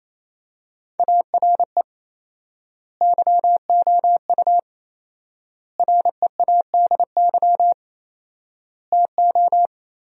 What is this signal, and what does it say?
Morse code, heard as a keyed tone, about 28 wpm, 700 Hz. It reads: ARE YOU READY TO
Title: 4x Word-Spacing - Sentences from Top 400 Words - 28wpm